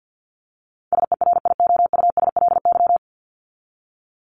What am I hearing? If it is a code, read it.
Morse code, heard as a keyed tone, about 51 wpm, 700 Hz. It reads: SERIOUSLY
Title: Top 1001-1250 Words 50wpm